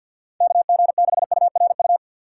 Morse code, heard as a keyed tone, about 52 wpm, 700 Hz. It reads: KG6URU
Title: USA Call Signs - 6 Characters - 4x Character Spacing - 50wpm